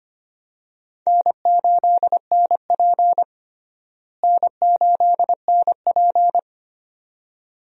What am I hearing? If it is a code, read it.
Morse code, heard as a keyed tone, about 25 wpm, 700 Hz. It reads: N8NP N8NP
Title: USA Call Signs - 4 Characters - Repeated 2x - 25wpm